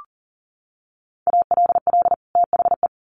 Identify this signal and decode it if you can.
Morse code, heard as a keyed tone, about 40 wpm, 700 Hz. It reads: ALL THE
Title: Sets of 2 Words 40wpm